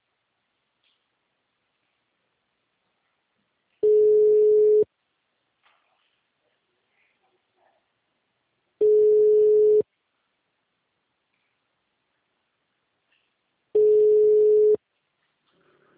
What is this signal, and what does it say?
Ringback tone (Europe, 425 Hz).